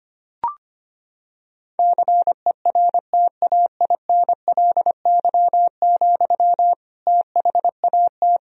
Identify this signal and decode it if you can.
Morse code, heard as a keyed tone, about 25 wpm, 700 Hz. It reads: CERTAINLY, THAT
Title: Sentences from Top 1000 Words - Review Part 2 - 25wpm